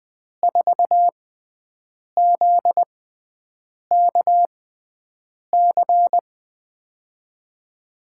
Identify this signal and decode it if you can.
Morse code, heard as a keyed tone, about 20 wpm, spaced as for 7 wpm, 700 Hz. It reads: 4ZKC
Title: USA Call Signs - 6 Characters - 6x Character Spacing - 20wpm